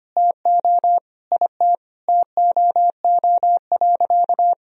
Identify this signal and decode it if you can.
Morse code, heard as a keyed tone, about 25 wpm, 700 Hz. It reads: TO IT TOO.
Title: Sentences from Top 1000 Words - Review Part 1 - 25wpm